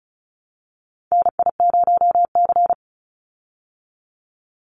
Morse code, heard as a keyed tone, about 34 wpm, 700 Hz. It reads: NI0C
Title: Call Signs Repeated 3x - 35wpm